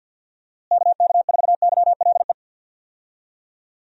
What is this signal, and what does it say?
Morse code, heard as a keyed tone, about 51 wpm, 700 Hz. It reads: KK4YLE
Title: USA Call Signs - 6 Characters - Repeated 2x - 50wpm